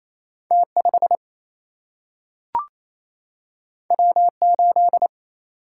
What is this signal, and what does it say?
Morse code, heard as a keyed tone, about 28 wpm, 700 Hz. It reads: T5 W8
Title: Single-Letter Single-Number 28wpm